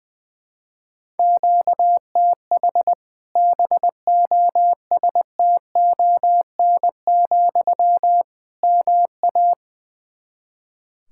Morse code, heard as a keyed tone, about 20 wpm, 700 Hz. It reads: QTH BOSTON, MA
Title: Practice QSO - Exchange 1 - Beginning - 20wpm